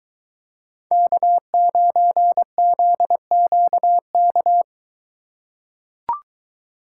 Morse code, heard as a keyed tone, about 22 wpm, 700 Hz. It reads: K9ZQK